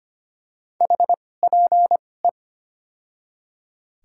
Morse code, heard as a keyed tone, about 25 wpm, spaced as for 18 wpm, 700 Hz. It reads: HPE
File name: Top 90 QSO Elements - 2x Character Spacing - 25wpm